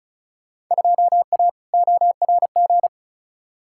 Morse code, heard as a keyed tone, about 36 wpm, 700 Hz. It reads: 2A ORG
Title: Field Day Exchanges Encore - 35wpm